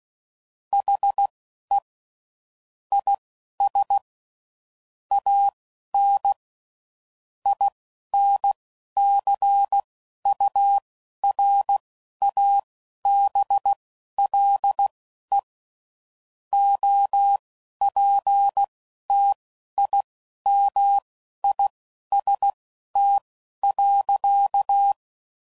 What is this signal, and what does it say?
Morse code, heard as a keyed tone, about 16 wpm, spaced as for 12 wpm, 800 Hz. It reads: HE IS AN INCURABLE OPTIMIST.
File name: Sayings / Quotes / Optimist